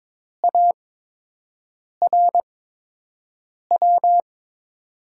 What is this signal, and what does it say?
Morse code, heard as a keyed tone, about 21 wpm, spaced as for 6 wpm, 700 Hz. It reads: ARW
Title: USA Call Signs - 8x Character Spacing - 22wpm